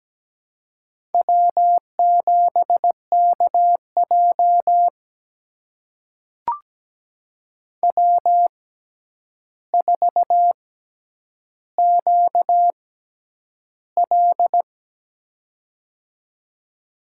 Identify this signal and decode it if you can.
Morse code, heard as a keyed tone, about 17 wpm, spaced as for 9 wpm, 700 Hz. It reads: W7KJ W4QL
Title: USA Call Signs - 6x Character Spacing - 17wpm